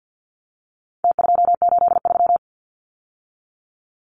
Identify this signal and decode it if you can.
Morse code, heard as a keyed tone, about 49 wpm, 700 Hz. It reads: T283
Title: Numbers - Contest Style - 4-Digit 50wpm